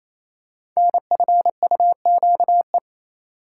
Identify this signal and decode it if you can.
Morse code, heard as a keyed tone, about 27 wpm, 690 Hz. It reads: NFUQE
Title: Code Groups - 5 Letter:Numbers - 28wpm